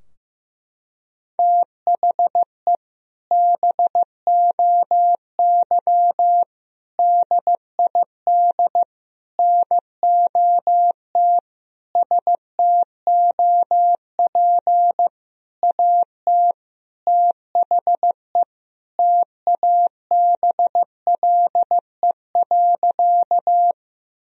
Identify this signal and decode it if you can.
Morse code, heard as a keyed tone, about 15 wpm, 700 Hz. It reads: THE BOY DID NOT STOP AT THE TABLE.
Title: Sentences from Top 1000 Words - Review Part 2 - 15wpm